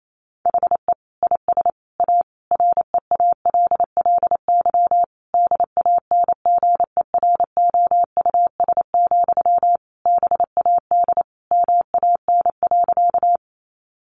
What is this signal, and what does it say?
Morse code, heard as a keyed tone, about 28 wpm, 700 Hz. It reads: HE IS A REALLY DANGEROUS, BAD MAN.
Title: Sentences from Top 1200 Words 28wpm